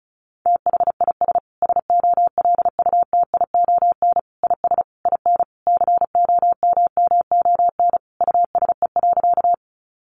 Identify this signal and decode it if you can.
Morse code, heard as a keyed tone, about 34 wpm, 700 Hz. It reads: THIS SOLUTION IS IN COMMON USE.
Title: Sentences from Top 1300 Words 35wpm